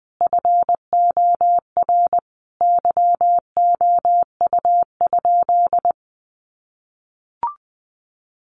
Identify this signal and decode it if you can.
Morse code, heard as a keyed tone, about 20 wpm, 700 Hz. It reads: FOR YOU?
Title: Sentences from Top 1400 Words 20wpm